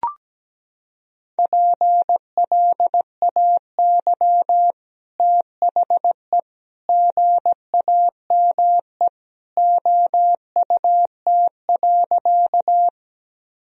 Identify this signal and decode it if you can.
Morse code, heard as a keyed tone, about 17 wpm, 700 Hz. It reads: PLAY THE GAME OUT.
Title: Sentences from Top 1000 Words - Review Part 3 - 17wpm